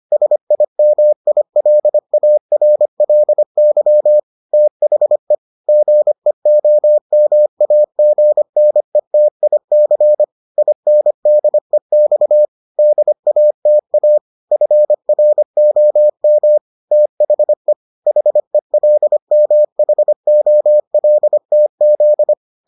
Morse code, heard as a keyed tone, about 25 wpm, 600 Hz. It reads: SIMILARLY THE GEOMAGNETIC INDEX DATA FROM THE HELMHOLTZ